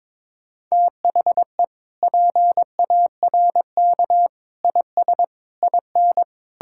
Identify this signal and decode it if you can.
Morse code, heard as a keyed tone, about 22 wpm, 700 Hz. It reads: THE PARK IS IN
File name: practice / 22wpm / Sentences with 4-Word Phrases - 22wpm